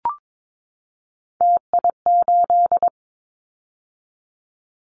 Morse code, heard as a keyed tone, about 22 wpm, 700 Hz. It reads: TI8
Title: Callsign Prefixes for Top 100 DXCC - 22wpm